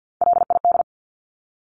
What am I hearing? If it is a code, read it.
Morse code, heard as a keyed tone, about 50 wpm, 700 Hz. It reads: LID
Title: Abbreviations 50wpm